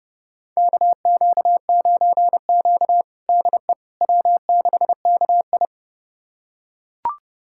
Morse code, heard as a keyed tone, about 30 wpm, 700 Hz. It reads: KQ9Q DE W6KI
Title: USA Callsign DE Callsign - 4 Characters - 30wpm